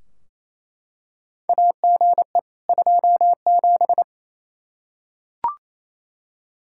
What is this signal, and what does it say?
Morse code, heard as a keyed tone, about 28 wpm, 700 Hz. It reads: AGE 27
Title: K1USN SST QSO Practice 28wpm